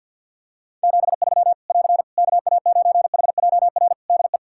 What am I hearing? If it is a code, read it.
Morse code, heard as a keyed tone, about 46 wpm, 700 Hz. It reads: 73 <AR> KA9HJR DE
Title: Practice QSO - Exchange 4 - Ending - 50wpm